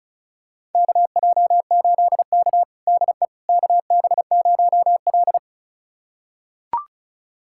Morse code, heard as a keyed tone, about 36 wpm, 690 Hz. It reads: KJ8K DE KB0L
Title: USA Callsign DE Callsign - 4 Characters - 35wpm